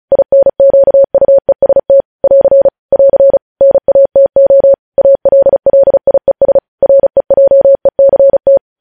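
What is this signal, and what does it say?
Morse code, heard as a keyed tone, about 36 wpm, 550 Hz. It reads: INQUEST <AR> <AR> NATO ALLIES REJECT
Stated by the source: Morse code news headlines at 35.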